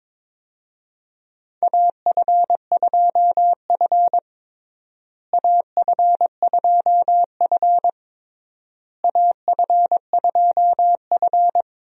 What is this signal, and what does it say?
Morse code, heard as a keyed tone, about 22 wpm, 700 Hz. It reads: AF2F AF2F AF2F
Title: USA Call Signs - 4 Characters - Repeated 3x - 22wpm